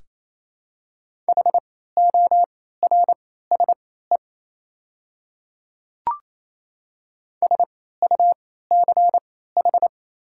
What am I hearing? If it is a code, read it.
Morse code, heard as a keyed tone, about 28 wpm, spaced as for 16 wpm, 700 Hz. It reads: HORSE SUCH
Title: 3x Character Spacing - Top 300 Words - 28wpm